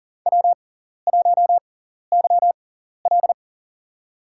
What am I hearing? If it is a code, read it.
Morse code, heard as a keyed tone, about 39 wpm, spaced as for 14 wpm, 700 Hz. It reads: W1YL
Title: USA Call Signs - 6x Character Spacing - 40wpm